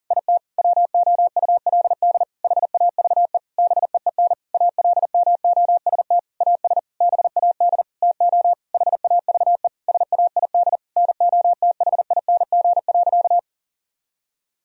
Morse code, heard as a keyed tone, about 40 wpm, 700 Hz. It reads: IT WOULD HAVE BEEN ALMOST AS BAD TO HAVE SAID NOTHING.
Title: Sentences from Top 1000 Words - Review Part 4 - 40wpm